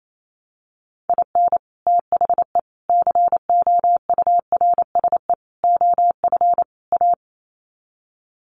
Morse code, heard as a keyed tone, about 28 wpm, 700 Hz. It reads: IN THE COURSE OF A
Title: Sets of 5 Words Encore2 28wpm